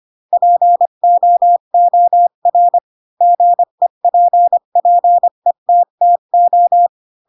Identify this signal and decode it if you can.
Morse code, heard as a keyed tone, about 25 wpm, 700 Hz. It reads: POOR GEPPETTO
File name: Pinocchio / Pinocchio ch3 20 25 ik1hge